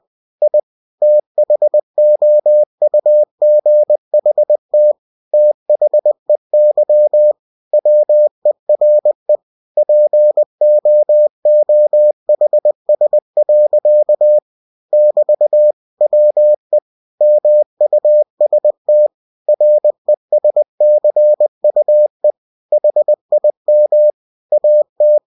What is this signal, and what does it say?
Morse code, heard as a keyed tone, about 20 wpm, 600 Hz. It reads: I THOUGHT THEY WERE POOHS. <BT> WE MUST RESCUE HIM AT
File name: cw-books / winnie-the-pooh / winnie-20wpm-page-0098